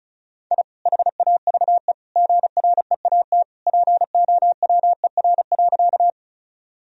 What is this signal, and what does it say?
Morse code, heard as a keyed tone, about 36 wpm, 700 Hz. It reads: I HAVE GREAT POWER.
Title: Sentences from Top 1000 Words - Review Part 2 - 35wpm